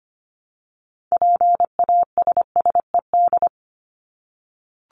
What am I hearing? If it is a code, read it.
Morse code, heard as a keyed tone, about 25 wpm, 700 Hz. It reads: PASSED
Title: ED Suffix - Top 100 Words with Base Words 25wpm